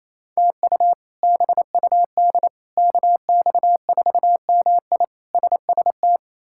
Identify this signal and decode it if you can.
Morse code, heard as a keyed tone, about 27 wpm, 700 Hz. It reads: TU BUD KX4MI SST
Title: K1USN SST QSO Practice - 28wpm